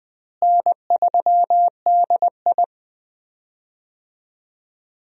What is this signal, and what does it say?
Morse code, heard as a keyed tone, about 21 wpm, 700 Hz. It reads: N3DI
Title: USA Call Signs - 4 Characters - Repeated 3x - 20wpm